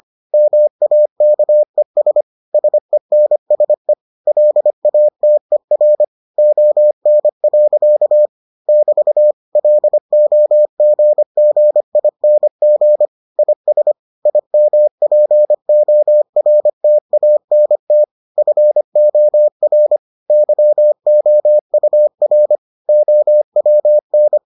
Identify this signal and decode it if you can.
Morse code, heard as a keyed tone, about 25 wpm, 600 Hz. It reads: MAKES SENSE LATER ON. <BT> LOGGING IS IMPORTANT FOR YOUR OWN